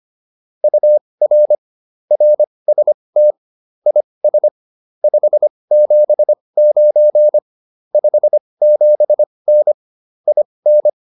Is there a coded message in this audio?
Yes: Morse code from 0.6 to 10.9 s, heard as a keyed tone, about 25 wpm, spaced as for 20 wpm, 600 Hz. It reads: UR RST IS 579 57N IN